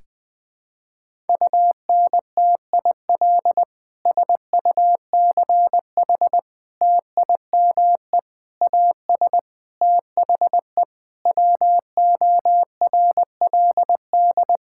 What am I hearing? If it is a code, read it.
Morse code, heard as a keyed tone, about 20 wpm, 700 Hz. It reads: UNTIL SUCH TIME AS THE WORLD
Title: Sentences with 9-Word Phrases - 20wpm